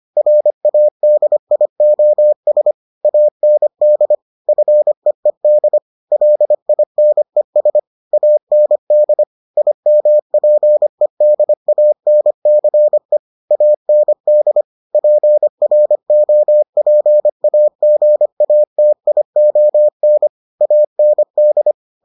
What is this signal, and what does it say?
Morse code, heard as a keyed tone, about 25 wpm, 600 Hz. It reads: RADIOS AND FEED LINES AND IMPEDANCE AND PROPAGATION AND